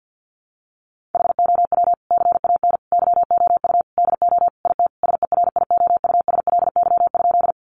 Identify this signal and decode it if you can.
Morse code, heard as a keyed tone, about 50 wpm, 700 Hz. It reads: HOW CAN YOU DO IT SERIOUSLY?
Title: Sentences from Top 1200 Words 50wpm